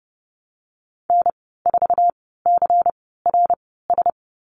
Morse code, heard as a keyed tone, about 30 wpm, spaced as for 17 wpm, 700 Hz. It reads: N4CRS
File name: Lesson 19g - Letter w - Call Signs 30wpm